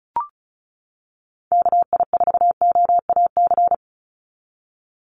Morse code, heard as a keyed tone, about 34 wpm, 700 Hz. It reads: KI4OAC
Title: USA Call Signs 35wpm